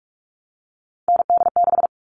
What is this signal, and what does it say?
Morse code, heard as a keyed tone, about 44 wpm, 700 Hz. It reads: ND6
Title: All Cut Numbers - 3-Digit 45wpm